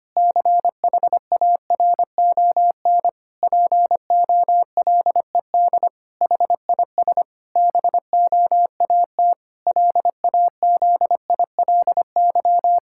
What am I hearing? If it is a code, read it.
Morse code, heard as a keyed tone, about 25 wpm, 700 Hz. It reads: CHARON POLED HIS BOAT LAZILY
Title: Pangram Challenge v2 - 25wpm